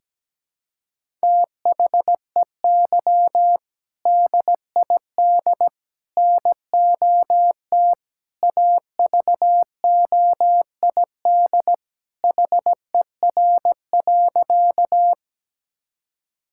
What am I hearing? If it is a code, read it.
Morse code, heard as a keyed tone, about 17 wpm, 700 Hz. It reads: THEY DID NOT AVOID HER.
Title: Sentences from Top 1000 Words - Review Part 6 - 17wpm